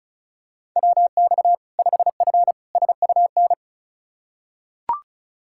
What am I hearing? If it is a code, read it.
Morse code, heard as a keyed tone, about 36 wpm, 700 Hz. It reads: WX 5F SUN
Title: Practice QSO - Exchange 2 - Info - 35wpm